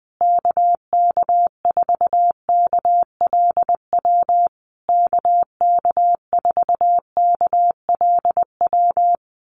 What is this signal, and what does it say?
Morse code, heard as a keyed tone, about 20 wpm, 700 Hz. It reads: KK4KLW KK4KLW
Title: USA Call Signs Repeated 3x 20wpm